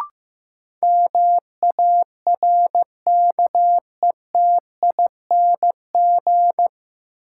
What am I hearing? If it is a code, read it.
Morse code, heard as a keyed tone, about 15 wpm, 700 Hz. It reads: MARKETING
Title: Sentences from Top 1000 Words - Review Part 5 - New Word Drill - 15wpm